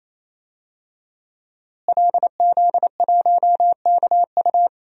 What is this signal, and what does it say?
Morse code, heard as a keyed tone, about 27 wpm, 700 Hz. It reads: LZ1KU
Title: DXCC-100 Call Signs - 5 Characters - Repeated 3x - 28wpm